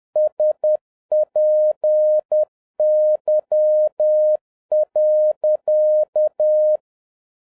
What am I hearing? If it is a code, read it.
Morse code, heard as a keyed tone, about 10 wpm, 610 Hz. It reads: SPY.